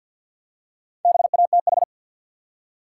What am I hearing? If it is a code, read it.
Morse code, heard as a keyed tone, about 53 wpm, 700 Hz. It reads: BATH
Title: Four Letter Words - 50wpm